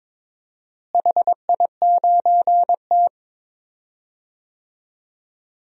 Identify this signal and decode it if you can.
Morse code, heard as a keyed tone, about 22 wpm, 700 Hz. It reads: HI9T
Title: DXCC-100 Call Signs - 4 Characters - Repeated 2x - 22wpm